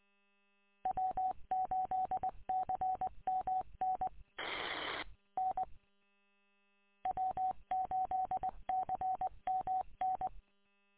Morse code, heard as a keyed tone, about 20 wpm, 750 Hz. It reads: W8CMN N W8CMN
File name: w8cmn- -combined